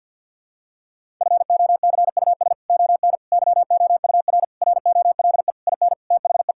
Morse code, heard as a keyed tone, about 52 wpm, 700 Hz. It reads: FOCUS ON YOUR ROLE IN THE
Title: Sentences from Top 1000 Words - Review Part 6 - 50wpm